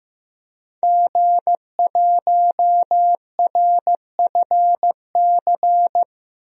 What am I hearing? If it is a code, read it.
Morse code, heard as a keyed tone, about 15 wpm, 700 Hz. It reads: G1RFC
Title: DXCC-100 Call Signs - 5 Characters - 15wpm